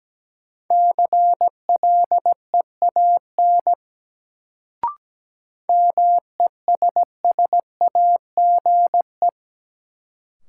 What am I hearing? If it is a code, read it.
Morse code, heard as a keyed tone, about 17 wpm, 700 Hz. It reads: CLEAN MESSAGE
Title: Sentences from Top 1100 Words - New Word Drill - 17wpm